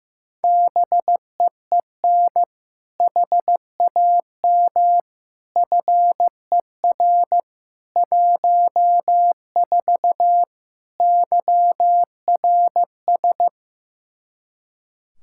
Morse code, heard as a keyed tone, about 15 wpm, 700 Hz. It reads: BEEN HAM FER 14 YRS
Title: Practice QSO - Exchange 3 - More Info - 15wpm